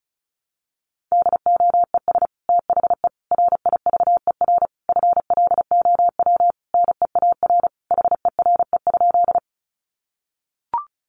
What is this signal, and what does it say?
Morse code, heard as a keyed tone, about 34 wpm, 700 Hz. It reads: DOES THE RIVER FLOW NEAR HERE?
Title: Sentences from Top 1200 Words 35wpm